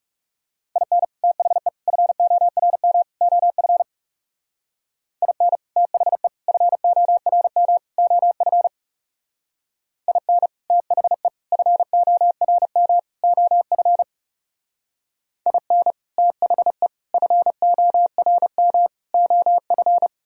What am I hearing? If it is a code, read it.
Morse code, heard as a keyed tone, about 36 wpm, 700 Hz. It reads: IN THE FORM OF IN THE FORM OF IN THE FORM OF IN THE FORM OF
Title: Speed-Racing - Sets of 4 Words 30wpm